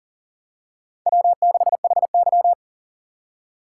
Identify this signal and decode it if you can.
Morse code, heard as a keyed tone, about 38 wpm, 690 Hz. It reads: W6HY
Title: USA Call Signs - 4 Characters - 8x Character Spacing - 40wpm